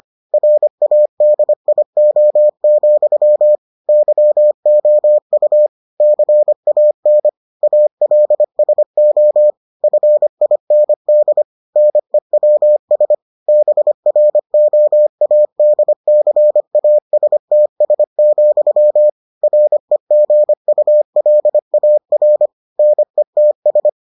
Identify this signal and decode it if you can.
Morse code, heard as a keyed tone, about 25 wpm, 600 Hz. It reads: RADIO, YOU CAN ALSO FIND NEWS BROADCASTS, REGULAR NETS